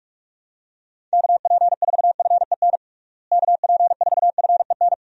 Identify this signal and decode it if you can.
Morse code, heard as a keyed tone, about 46 wpm, 700 Hz. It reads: KP4FEN KP4FEN
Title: DXCC-100 Call Signs - 6 Characters - Repeated 2x - 45wpm